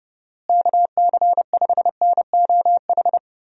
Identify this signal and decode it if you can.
Morse code, heard as a keyed tone, about 30 wpm, 700 Hz. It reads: KC5NOH
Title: USA Call Signs - 6 Characters - Repeated 3x - 30wpm